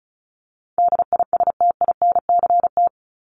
Morse code, heard as a keyed tone, about 34 wpm, 700 Hz. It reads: DISTINCT
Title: Top 1501-1750 Words 35wpm